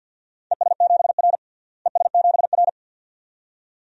Morse code, heard as a keyed tone, about 47 wpm, 700 Hz. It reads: ES7R ES7R
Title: DXCC-100 Call Signs - 4 Characters - Repeated 2x - 50wpm